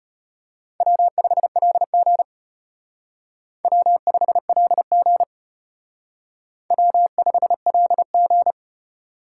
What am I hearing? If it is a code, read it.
Morse code, heard as a keyed tone, about 31 wpm, 700 Hz. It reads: W5LG W5LG W5LG
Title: Speed Racing - USA Call Signs 25wpm